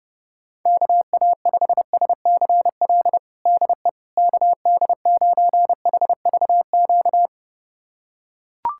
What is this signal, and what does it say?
Morse code, heard as a keyed tone, about 30 wpm, 700 Hz. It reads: KA5SCL DE KD9HVQ
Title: USA Callsign DE Callsign - 6 Characters - 30wpm